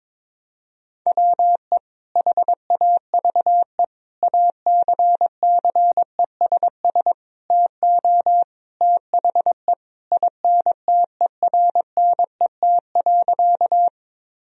Morse code, heard as a keyed tone, about 21 wpm, 700 Hz. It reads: WE HAVE ACCESS TO THE INTERNET.
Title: Sentences from Top 1000 Words - Review Part 4 - 22wpm